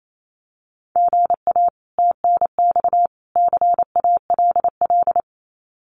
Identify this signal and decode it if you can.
Morse code, heard as a keyed tone, about 28 wpm, 700 Hz. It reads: GA TNX CALL
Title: Common QSO Phrases 28wpm